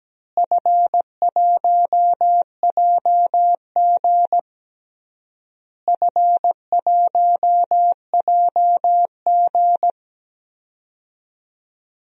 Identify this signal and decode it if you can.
Morse code, heard as a keyed tone, about 17 wpm, 700 Hz. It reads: F1JG F1JG
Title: DXCC-100 Call Signs - 4 Characters - Repeated 3x - 17wpm